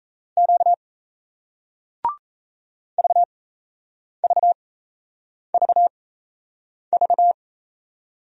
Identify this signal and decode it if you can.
Morse code, heard as a keyed tone, about 38 wpm, spaced as for 8 wpm, 700 Hz. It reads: Q VVVV